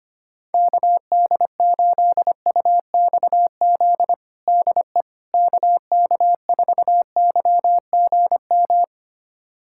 Morse code, heard as a keyed tone, about 25 wpm, 700 Hz. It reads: KD8UXZ DE KK4YGM
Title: USA Callsign DE Callsign - 6 Characters - 25wpm